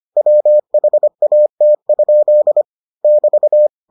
Morse code, heard as a keyed tone, about 25 wpm, 600 Hz. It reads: WHAT? <BT>